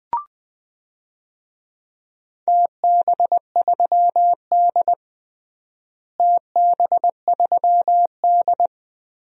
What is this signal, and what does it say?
Morse code, heard as a keyed tone, about 20 wpm, 700 Hz. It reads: TB3D TB3D
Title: DXCC-100 Call Signs - 4 Characters - Repeated 2x - 20wpm